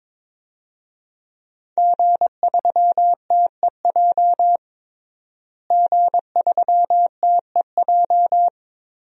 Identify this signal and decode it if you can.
Morse code, heard as a keyed tone, about 22 wpm, 700 Hz. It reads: G3TEJ G3TEJ
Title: DXCC-100 Call Signs - 5 Characters - Repeated 3x - 22wpm